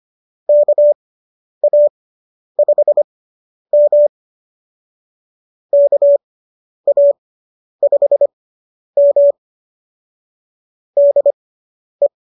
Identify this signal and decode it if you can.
Morse code, heard as a keyed tone, about 25 wpm, spaced as for 10 wpm, 570 Hz. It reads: KA5M KA5M DE